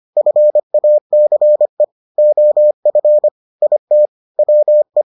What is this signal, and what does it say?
Morse code, heard as a keyed tone, about 25 wpm, 600 Hz. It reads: FACE OF IT WE